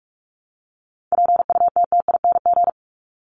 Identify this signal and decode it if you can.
Morse code, heard as a keyed tone, about 44 wpm, 700 Hz. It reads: PUTTING
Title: ING Suffix - Top 100 Words with Base Words 45wpm